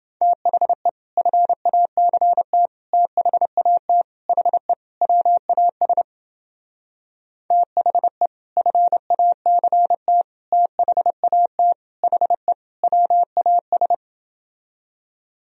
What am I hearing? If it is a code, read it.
Morse code, heard as a keyed tone, about 31 wpm, 700 Hz. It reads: THE FACT THAT HE WAS THE FACT THAT HE WAS